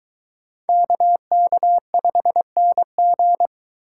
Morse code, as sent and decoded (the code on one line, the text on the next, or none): -.- -.- ..... -. --.
KK5NG